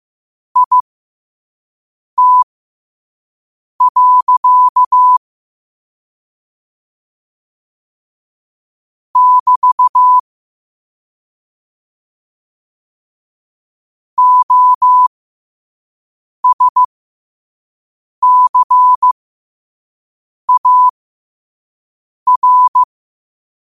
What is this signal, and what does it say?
Morse code, heard as a keyed tone, about 15 wpm, spaced as for 5 wpm, 1000 Hz. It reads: IT. <BT> OSCAR
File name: qotdfiles / 5wpm